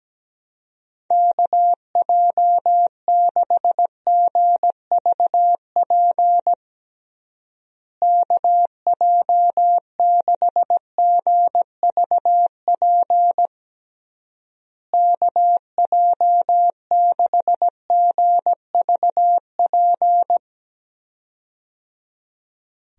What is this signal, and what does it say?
Morse code, heard as a keyed tone, about 17 wpm, 700 Hz. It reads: KJ6GVP KJ6GVP KJ6GVP
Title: USA Call Signs - 6 Characters - Repeated 3x - 17wpm